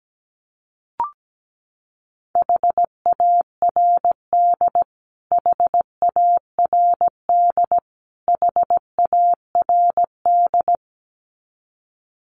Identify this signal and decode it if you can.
Morse code, heard as a keyed tone, about 17 wpm, 700 Hz. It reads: HARD HARD HARD
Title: Top 500 Words Repeated 17wpm